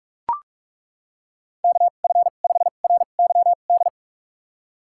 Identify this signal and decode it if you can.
Morse code, heard as a keyed tone, about 42 wpm, spaced as for 31 wpm, 700 Hz. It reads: KF5RYD
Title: USA Call Signs - 6 Characters - 2x Character Spacing - 45wpm